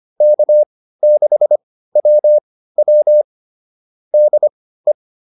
Morse code, heard as a keyed tone, about 25 wpm, spaced as for 15 wpm, 600 Hz. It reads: K6WW DE